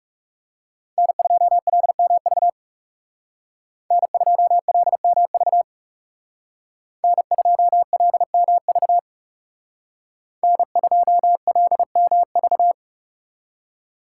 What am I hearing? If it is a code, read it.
Morse code, heard as a keyed tone, about 38 wpm, 700 Hz. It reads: N2LMV N2LMV N2LMV N2LMV